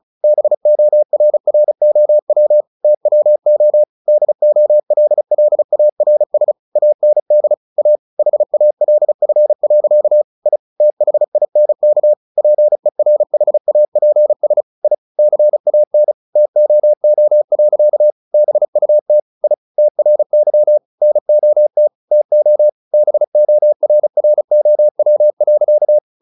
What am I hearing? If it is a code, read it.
Morse code, heard as a keyed tone, about 36 wpm, 600 Hz. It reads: BORROW TWO DOLLARS AND A HALF. I THINK PERHAPS I CAN TOO. BUT I TRY NOT TO BORROW.
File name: the old man and the sea / the old man and the sea 1 35wpm